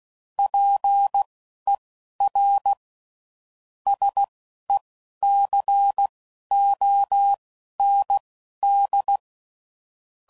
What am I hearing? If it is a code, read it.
Morse code, heard as a keyed tone, about 16 wpm, spaced as for 11 wpm, 800 Hz. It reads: PER SECOND
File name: CW Code Groups @10WPM / Technical Subjects / XL